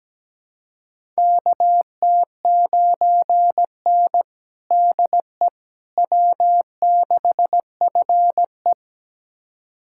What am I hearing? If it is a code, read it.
Morse code, heard as a keyed tone, about 17 wpm, 700 Hz. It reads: KT9N DE W6FE
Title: USA Callsign DE Callsign - 4 Characters - 17wpm